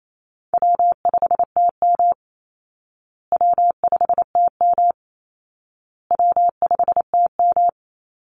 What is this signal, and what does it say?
Morse code, heard as a keyed tone, about 28 wpm, 700 Hz. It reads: W5TM W5TM W5TM
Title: Easy Call Signs Repeated 3x - 28wpm